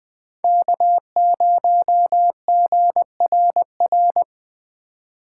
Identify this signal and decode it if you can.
Morse code, heard as a keyed tone, about 20 wpm, 700 Hz. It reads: K0GRR
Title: Speed Racing - USA 1x3 Call Signs 17wpm